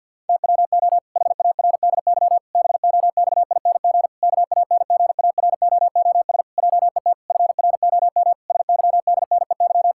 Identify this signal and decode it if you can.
Morse code, heard as a keyed tone, about 50 wpm, 700 Hz. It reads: TWO HARDY BOXING KANGAROOS JET FROM SYDNEY
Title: Pangram Challenge v2 - 50wpm